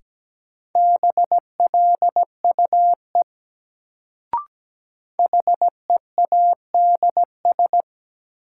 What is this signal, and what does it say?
Morse code, heard as a keyed tone, about 17 wpm, 700 Hz. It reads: BLUE HEADS
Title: Sentences from Top 1000 Words - Review Part 3 - New Word Drill - 17wpm